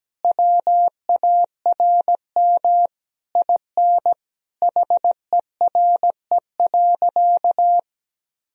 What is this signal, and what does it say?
Morse code, heard as a keyed tone, about 17 wpm, 700 Hz. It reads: WARM IN HERE.
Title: Sentences from Top 1000 Words - Review Part 4 - 17wpm